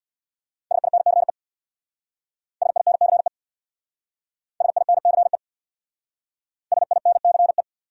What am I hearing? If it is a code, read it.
Morse code, heard as a keyed tone, about 60 wpm, 700 Hz. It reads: SINCE SINCE SINCE SINCE
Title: Speed-Racing Top 300 Words 50wpm